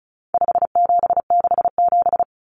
Morse code, heard as a keyed tone, about 34 wpm, 700 Hz. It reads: 5767
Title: Common Cut Numbers - 4-Digit 35wpm